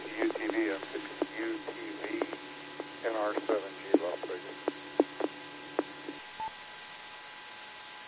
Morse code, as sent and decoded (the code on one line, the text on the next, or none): none